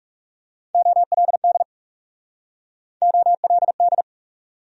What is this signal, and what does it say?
Morse code, heard as a keyed tone, about 42 wpm, 700 Hz. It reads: OLD OLD